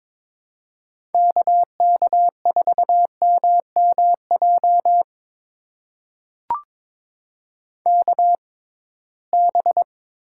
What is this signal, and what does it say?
Morse code, heard as a keyed tone, about 22 wpm, spaced as for 17 wpm, 700 Hz. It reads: KK4MMJ KB
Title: USA Call Signs - 6x Character Spacing - 22wpm